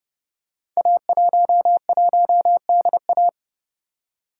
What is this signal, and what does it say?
Morse code, heard as a keyed tone, about 30 wpm, 700 Hz. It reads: A11DA
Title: Code Groups - 5 Letter:Numbers - 30wpm